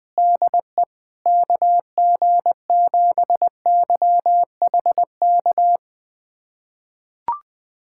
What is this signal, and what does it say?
Morse code, heard as a keyed tone, about 20 wpm, 700 Hz. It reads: DE KG7YHK
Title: USA Callsign DE Callsign 20wpm